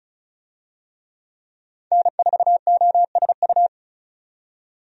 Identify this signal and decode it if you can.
Morse code, heard as a keyed tone, about 36 wpm, 700 Hz. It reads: N4OSU
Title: USA Call Signs - 5 Characters - 35wpm